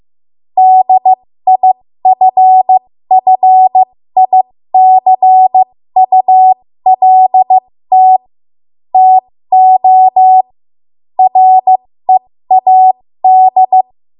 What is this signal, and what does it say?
Morse code, heard as a keyed tone, about 15 wpm, spaced as for 13 wpm, 750 Hz. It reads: DIFFICULT TO READ
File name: Morse / 13WPM